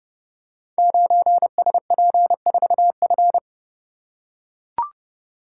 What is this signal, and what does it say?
Morse code, heard as a keyed tone, about 29 wpm, 700 Hz. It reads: 9SP4F
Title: Code Groups - 5 Letter:Numbers - 30wpm